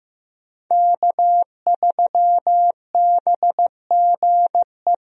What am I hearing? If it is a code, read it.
Morse code, heard as a keyed tone, about 15 wpm, 700 Hz. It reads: K3BGE
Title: USA Call Signs - 5 Characters - Repeated 2x - 15wpm